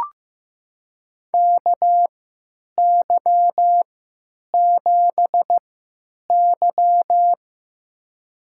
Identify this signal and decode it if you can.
Morse code, heard as a keyed tone, about 15 wpm, spaced as for 8 wpm, 700 Hz. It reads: KY7Y
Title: USA Call Signs - 3x Character Spacing - 15wpm